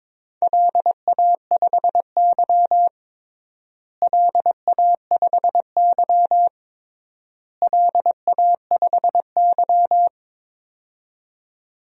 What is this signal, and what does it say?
Morse code, heard as a keyed tone, about 22 wpm, 700 Hz. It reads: LA5Y LA5Y LA5Y
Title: DXCC-100 Call Signs - 4 Characters - Repeated 3x - 22wpm